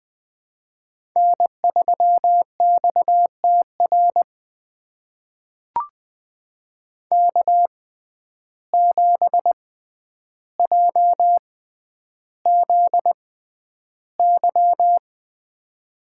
Morse code, heard as a keyed tone, about 20 wpm, spaced as for 10 wpm, 700 Hz. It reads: N3XTR K7JZY
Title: USA Call Signs - 5 Characters - 6x Character Spacing - 20wpm